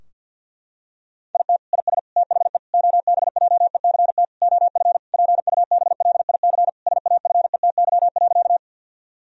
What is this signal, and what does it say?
Morse code, heard as a keyed tone, about 50 wpm, 700 Hz. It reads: IT IS THE OBJECT OF PUBLIC SAFETY.
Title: Sentences from Top 1000 Words - Review Part 4 - 50wpm